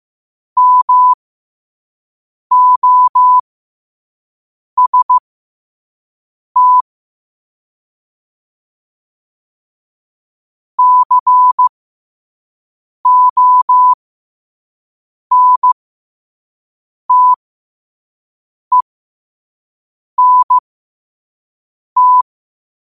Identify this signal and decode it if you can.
Morse code, heard as a keyed tone, about 15 wpm, spaced as for 5 wpm, 1000 Hz. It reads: MOST CONTENT
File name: qotdfiles / 5wpm